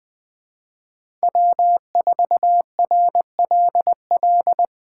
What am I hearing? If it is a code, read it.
Morse code, heard as a keyed tone, about 20 wpm, 700 Hz. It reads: W4RLL
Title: Speed Racing - USA 1x3 Call Signs 17wpm